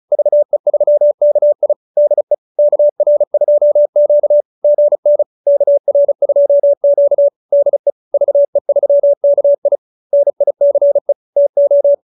Morse code, heard as a keyed tone, about 34 wpm, 590 Hz. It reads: VE3KI DE KR2Q GN KR2Q DE VE3KI NICE TO